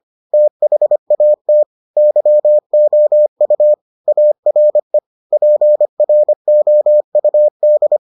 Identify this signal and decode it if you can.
Morse code, heard as a keyed tone, about 25 wpm, 600 Hz. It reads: THAT YOU ARE PROUD